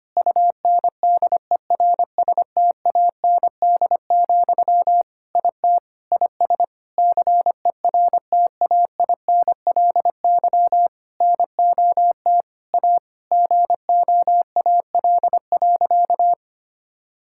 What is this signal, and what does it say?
Morse code, heard as a keyed tone, about 25 wpm, 700 Hz. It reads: UNDERSTAND, IT IS CERTAINLY NOT A GOAL.
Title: Sentences from Top 1000 Words - Review Part 5 - 25wpm